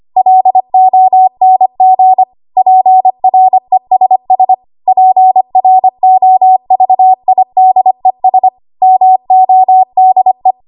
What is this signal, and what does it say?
Morse code, heard as a keyed tone, about 25 wpm, 750 Hz. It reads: LONG PRESS PROVIDES MODE
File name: Archive / 25 WPM / 25WPM